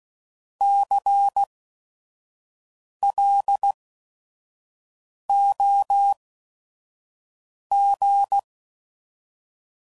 Morse code, heard as a keyed tone, about 16 wpm, spaced as for 5 wpm, 800 Hz. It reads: CLOG
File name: Prefixes / By First Letter / Cs